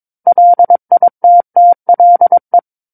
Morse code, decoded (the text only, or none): LITTLE